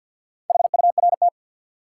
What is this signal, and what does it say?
Morse code, heard as a keyed tone, about 46 wpm, 700 Hz. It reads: HURT
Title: Four Letter Words - 50wpm